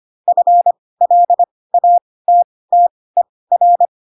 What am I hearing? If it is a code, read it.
Morse code, heard as a keyed tone, about 25 wpm, spaced as for 18 wpm, 700 Hz. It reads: FLATTER
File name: Pinocchio / Pinocchio ch2 12 25 ik1hge